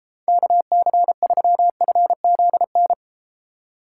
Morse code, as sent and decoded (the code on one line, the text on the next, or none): -.- -.-. ...-- ..-. --.. -.
KC3FZN